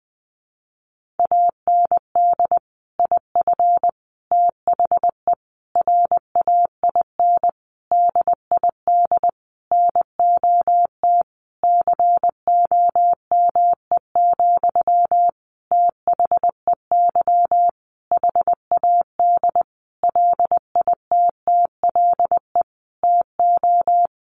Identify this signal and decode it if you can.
Morse code, heard as a keyed tone, about 20 wpm, 700 Hz. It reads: AND IF THE RAIN DID NOT COME, THEY HAD LITTLE TO
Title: Sentences from Top 1300 Words 20wpm